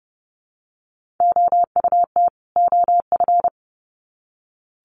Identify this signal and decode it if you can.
Morse code, heard as a keyed tone, about 30 wpm, 700 Hz. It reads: OUT OF
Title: Sets of 2 Words 30wpm